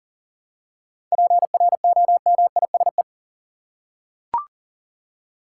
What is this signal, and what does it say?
Morse code, heard as a keyed tone, about 37 wpm, 700 Hz. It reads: PROMISE
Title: Seven Letter Words - 40wpm